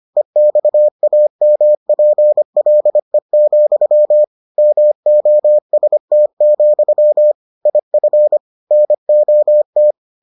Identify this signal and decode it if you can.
Morse code, heard as a keyed tone, about 25 wpm, 600 Hz. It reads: EXAMPLE, MOST, IF NOT